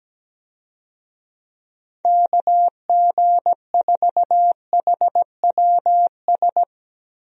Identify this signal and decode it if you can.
Morse code, heard as a keyed tone, about 17 wpm, 700 Hz. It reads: KG4HWS
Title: USA Call Signs - 6 Characters - Repeated 2x - 17wpm